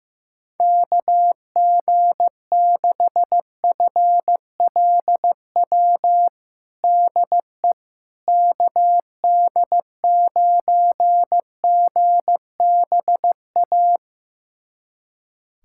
Morse code, heard as a keyed tone, about 15 wpm, 700 Hz. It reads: KG6FLW DE KD9GBA
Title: USA Callsign DE Callsign 15wpm